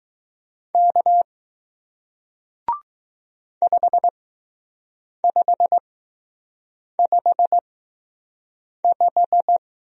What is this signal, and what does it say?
Morse code, heard as a keyed tone, about 20 wpm, spaced as for 6 wpm, 700 Hz. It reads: K 5555